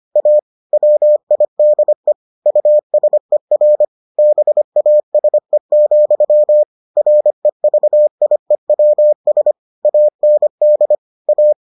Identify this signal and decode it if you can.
Morse code, heard as a keyed tone, about 25 wpm, 600 Hz. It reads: A WIDE USER BASE, REVIEWS AND A